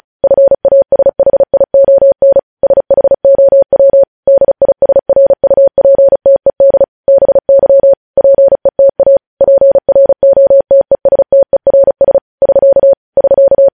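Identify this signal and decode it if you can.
Morse code, heard as a keyed tone, about 36 wpm, 550 Hz. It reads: FASHION SHOW DISRUPTED BY PETA PROTESTERS <SK> <SK>
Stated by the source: Morse code news headlines sent at 35.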